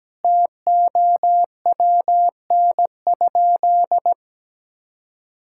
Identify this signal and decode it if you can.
Morse code, heard as a keyed tone, about 17 wpm, 700 Hz. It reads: TOWN?
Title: Sentences from Top 1000 Words - Review Part 6 - 17wpm